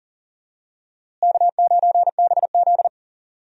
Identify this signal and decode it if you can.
Morse code, heard as a keyed tone, about 40 wpm, 700 Hz. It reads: K9BZ
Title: USA Call Signs - 4 Characters - Repeated 3x - 40wpm